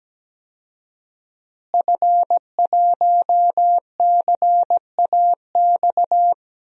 Morse code, heard as a keyed tone, about 17 wpm, 700 Hz. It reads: F1CAX
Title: DXCC-100 Call Signs - 5 Characters - Repeated 2x - 17wpm